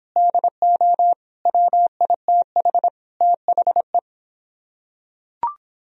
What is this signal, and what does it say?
Morse code, heard as a keyed tone, about 26 wpm, 700 Hz. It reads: DO WITH THE